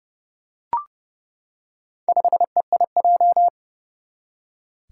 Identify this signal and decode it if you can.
Morse code, heard as a keyed tone, about 30 wpm, 700 Hz. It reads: 5EIJ
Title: Four Challenging Letter-Number Combinations 30wpm